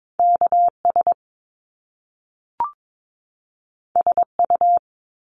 Morse code, heard as a keyed tone, about 22 wpm, 700 Hz. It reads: KS SU
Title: Top Two Letter Combinations in Words 22wpm